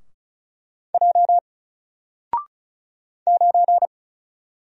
Morse code, heard as a keyed tone, about 36 wpm, 700 Hz. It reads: J 9
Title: Single Character (Letter:Number:Punctuation:Prosign) 35wpm